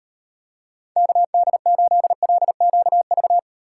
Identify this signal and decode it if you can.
Morse code, heard as a keyed tone, about 38 wpm, 700 Hz. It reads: KD8LQV